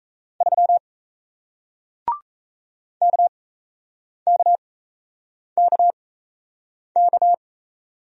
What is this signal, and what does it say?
Morse code, heard as a keyed tone, about 38 wpm, spaced as for 8 wpm, 700 Hz. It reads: 3 KKKK